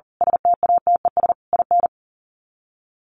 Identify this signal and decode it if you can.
Morse code, heard as a keyed tone, about 40 wpm, 700 Hz. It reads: STATES IN
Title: Sets of 4 Words 40wpm